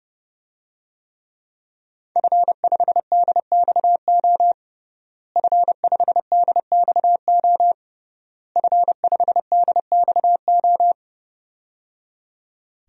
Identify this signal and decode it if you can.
Morse code, heard as a keyed tone, about 29 wpm, 700 Hz. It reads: F5DXO F5DXO F5DXO
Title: DXCC-100 Call Signs - 5 Characters - Repeated 3x - 30wpm